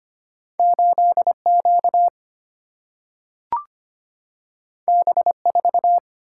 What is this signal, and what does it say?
Morse code, heard as a keyed tone, about 25 wpm, 700 Hz. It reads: 8Q B4
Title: Single-Letter Single-Number 25wpm